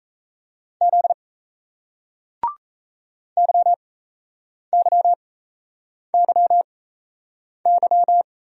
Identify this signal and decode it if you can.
Morse code, heard as a keyed tone, about 37 wpm, spaced as for 8 wpm, 700 Hz. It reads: Z YYYY